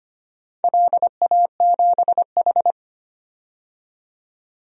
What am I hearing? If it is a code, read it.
Morse code, heard as a keyed tone, about 25 wpm, 700 Hz. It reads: LA7H
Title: DXCC-100 Call Signs - 4 Characters - 25wpm